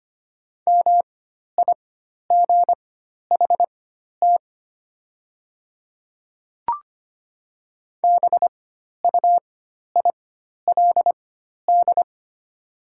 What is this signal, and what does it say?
Morse code, heard as a keyed tone, about 25 wpm, spaced as for 12 wpm, 700 Hz. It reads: MIGHT BUILD
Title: 4x Character Spacing - Top 300 Words - 25wpm